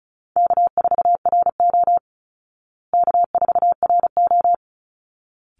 Morse code, heard as a keyed tone, about 34 wpm, 700 Hz. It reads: K4RO K4RO
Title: Easy Call Signs Repeated 2x - 35wpm